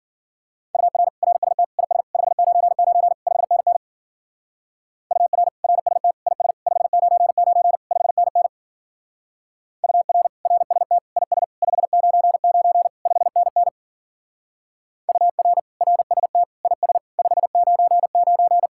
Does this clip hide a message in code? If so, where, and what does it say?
Yes: Morse code from 0.7 to 18.7 s, heard as a keyed tone, about 52 wpm, 700 Hz. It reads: UR RST IS 599 5NN UR RST IS 599 5NN UR RST IS 599 5NN UR RST IS 599